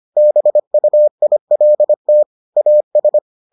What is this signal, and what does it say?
Morse code, heard as a keyed tone, about 25 wpm, 600 Hz. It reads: BUILT AS